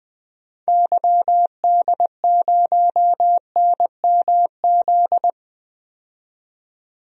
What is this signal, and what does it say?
Morse code, heard as a keyed tone, about 20 wpm, 700 Hz. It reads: YD0NMZ
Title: DXCC-100 Call Signs - 6 Characters - 20wpm